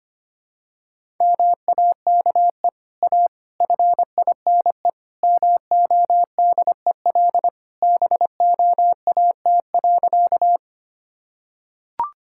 Morse code, heard as a keyed tone, about 25 wpm, 700 Hz. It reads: MAKE A FINE MODEL BOAT.
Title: Sentences from Top 1000 Words - Review Part 5 - 25wpm